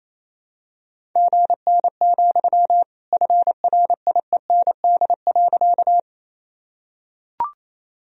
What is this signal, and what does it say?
Morse code, heard as a keyed tone, about 28 wpm, 700 Hz. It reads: GN, FRIEND.
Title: Sentences from Top 300 Words and Abbreviations 28wpm